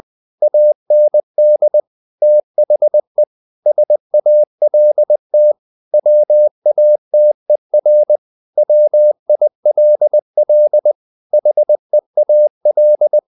Morse code, heard as a keyed tone, about 20 wpm, 600 Hz. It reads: AND THE SALT WATER WILL HEAL
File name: the old man and the sea part 2 20wpm